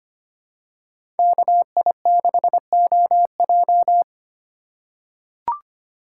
Morse code, heard as a keyed tone, about 25 wpm, 700 Hz. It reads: KI6OJ
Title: USA Call Signs - 5 Characters - 4x Character Spacing - 25wpm